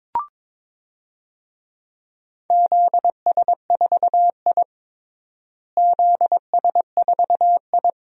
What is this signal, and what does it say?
Morse code, heard as a keyed tone, about 22 wpm, 700 Hz. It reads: ZS4I ZS4I
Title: DXCC-100 Call Signs - 4 Characters - Repeated 3x - 22wpm